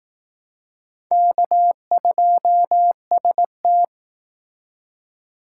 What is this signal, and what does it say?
Morse code, heard as a keyed tone, about 18 wpm, 700 Hz. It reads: K2ST
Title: Speed Racing - USA Call Signs 15wpm